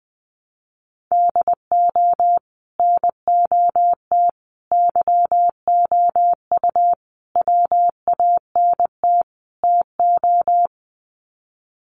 Morse code, heard as a keyed tone, about 20 wpm, 700 Hz. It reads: DO NOT YOU WANT TO
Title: Sets of 5 Words Encore2 20wpm